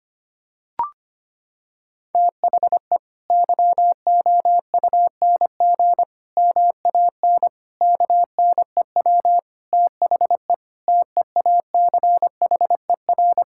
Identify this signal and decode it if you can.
Morse code, heard as a keyed tone, about 25 wpm, 700 Hz. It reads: THE YOUNG MAN KNEW THE TEACHER
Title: Sentences from Top 1000 Words - Review Part 6 - 25wpm